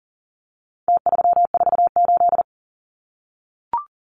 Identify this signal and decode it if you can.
Morse code, heard as a keyed tone, about 40 wpm, 700 Hz. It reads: T348
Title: Numbers - Contest Style - 4-Digit 40wpm